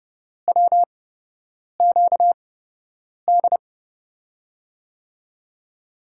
Morse code, heard as a keyed tone, about 30 wpm, spaced as for 8 wpm, 700 Hz. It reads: WQD
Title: USA Call Signs - 5 Characters - 8x Character Spacing - 30wpm